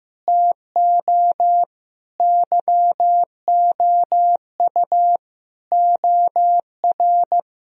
Morse code, heard as a keyed tone, about 15 wpm, 700 Hz. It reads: TO YOU OR
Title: Sentences from Top 1000 Words - Review Part 1 - 15wpm